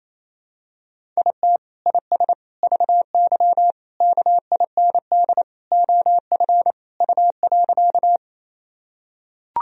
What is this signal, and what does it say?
Morse code, heard as a keyed tone, about 28 wpm, 700 Hz. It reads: IT IS VY KIND OF U.
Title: Sentences from Top 200 Words and Abbreviations 28wpm